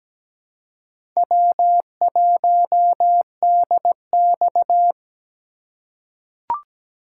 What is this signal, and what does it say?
Morse code, heard as a keyed tone, about 17 wpm, 700 Hz. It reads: W1DX
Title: K1USN SST QSO Practice - 17wpm